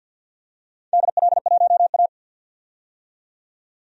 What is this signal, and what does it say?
Morse code, heard as a keyed tone, about 50 wpm, 700 Hz. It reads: DL1A
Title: DXCC-100 Call Signs - 4 Characters - 50wpm